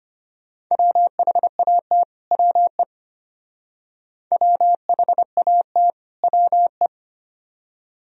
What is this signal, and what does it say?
Morse code, heard as a keyed tone, about 31 wpm, 700 Hz. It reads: WHAT WE WHAT WE
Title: Speed-Racing - Sets of 2 Words 25wpm